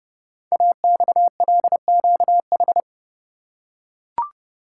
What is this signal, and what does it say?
Morse code, heard as a keyed tone, about 29 wpm, 700 Hz. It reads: AXLQH
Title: Code Groups - 5 Letters - 30wpm